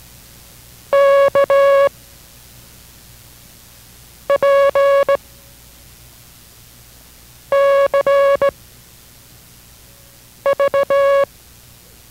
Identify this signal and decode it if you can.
Morse code, heard as a keyed tone, about 18 wpm, spaced as for 4 wpm, 560 Hz. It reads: KPCV